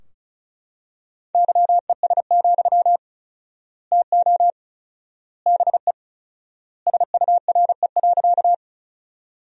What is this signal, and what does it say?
Morse code, heard as a keyed tone, about 36 wpm, 700 Hz. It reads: YES, TO BE SURE.
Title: 4x Word-Spacing - Sentences from Top 500 Words - 35wpm